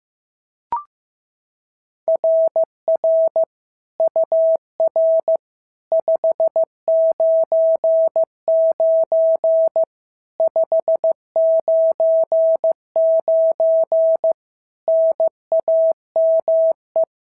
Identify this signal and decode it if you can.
Morse code, heard as a keyed tone, about 15 wpm, 650 Hz. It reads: RR UR 599 599 NAME
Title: SOTA QSO Practice 15wpm